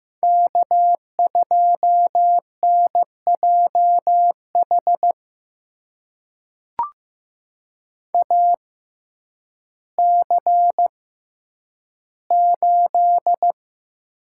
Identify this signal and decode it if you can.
Morse code, heard as a keyed tone, about 15 wpm, spaced as for 9 wpm, 700 Hz. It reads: K2NJH AC8
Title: USA Call Signs - 5 Characters - 6x Character Spacing - 15wpm